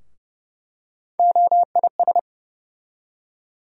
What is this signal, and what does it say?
Morse code, heard as a keyed tone, about 31 wpm, 700 Hz. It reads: OIS
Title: USA Call Sign Postfixes - 3 Letters 30wpm